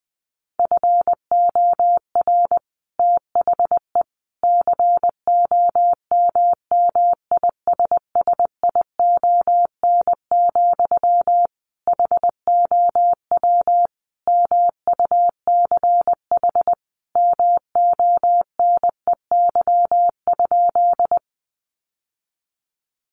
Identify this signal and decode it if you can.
Morse code, heard as a keyed tone, about 20 wpm, 700 Hz. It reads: FOR THE COMMISSION, HOW MUCH MONEY?
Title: Sentences from Top 1400 Words 20wpm